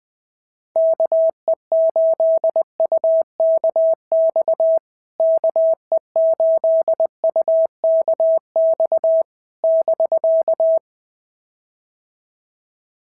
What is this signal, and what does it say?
Morse code, heard as a keyed tone, about 19 wpm, 650 Hz. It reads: KE8UKX KE8UKX <BK>
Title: POTA QSO Practice - 20wpm